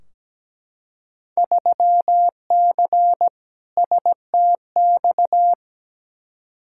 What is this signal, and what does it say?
Morse code, heard as a keyed tone, about 17 wpm, 700 Hz. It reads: 3C STX
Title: Field Day Exchanges Encore - 17wpm